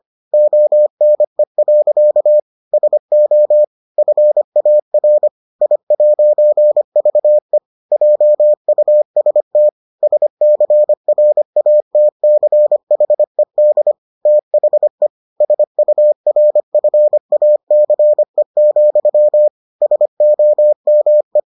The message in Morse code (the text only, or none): ONE. SO FAR I'VE JUST SCRATCHED THE SURFACE, SOME